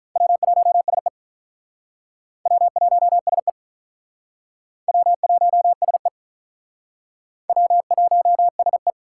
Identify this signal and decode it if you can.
Morse code, heard as a keyed tone, about 44 wpm, 700 Hz. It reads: W1SE W1SE W1SE W1SE